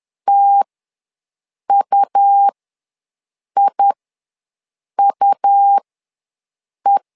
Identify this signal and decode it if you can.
Morse code, heard as a keyed tone, about 11 wpm, spaced as for 6 wpm, 800 Hz. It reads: TUIUE
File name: LEZIONI / 2 UHDN / 2 U ws 10wpm 2017